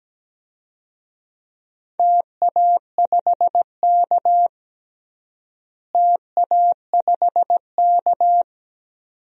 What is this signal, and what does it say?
Morse code, heard as a keyed tone, about 17 wpm, 700 Hz. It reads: TA5K TA5K
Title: DXCC-100 Call Signs - 4 Characters - Repeated 2x - 17wpm